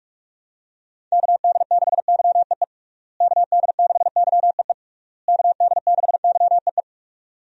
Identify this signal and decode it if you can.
Morse code, heard as a keyed tone, about 46 wpm, 700 Hz. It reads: KD6YEE KD6YEE KD6YEE
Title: USA Call Signs - 6 Characters - Repeated 3x - 45wpm